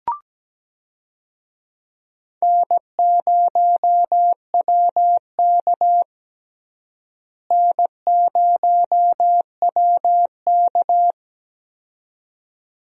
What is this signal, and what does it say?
Morse code, heard as a keyed tone, about 17 wpm, 700 Hz. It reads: N0WK N0WK
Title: USA Call Signs - 4 Characters - Repeated 2x - 17wpm